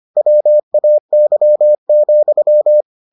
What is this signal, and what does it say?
Morse code, heard as a keyed tone, about 25 wpm, 600 Hz. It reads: WAY,